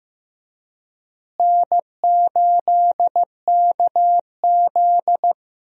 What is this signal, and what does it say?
Morse code, heard as a keyed tone, about 15 wpm, 700 Hz. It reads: N8KZ
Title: USA Call Signs - 4 Characters - Repeated 2x - 15wpm